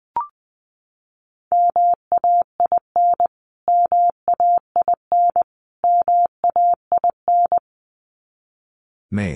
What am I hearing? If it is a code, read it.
Morse code, heard as a keyed tone, about 20 wpm, 700 Hz. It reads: MAIN MAIN MAIN
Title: Top 500 Words Repeated 20wpm